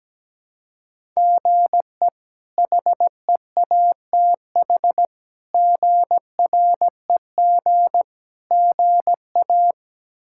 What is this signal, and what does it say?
Morse code, heard as a keyed tone, about 17 wpm, 700 Hz. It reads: GE HEATH GREG GA
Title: K1USN SST QSO Practice - 17wpm